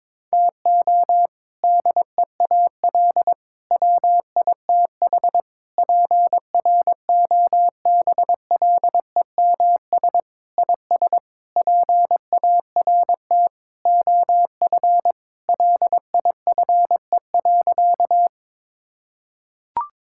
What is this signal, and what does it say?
Morse code, heard as a keyed tone, about 22 wpm, 700 Hz. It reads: TO DEAL WITH PROBLEMS IS PART OF LIFE.
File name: practice / 22wpm / Sentences with 3-Word Phrases - 22wpm